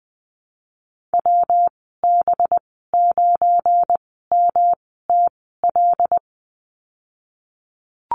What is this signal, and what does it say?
Morse code, heard as a keyed tone, about 20 wpm, spaced as for 15 wpm, 700 Hz. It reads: WB9MTL
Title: 193 Lesson 31g - Letter v - Call Signs 20wpm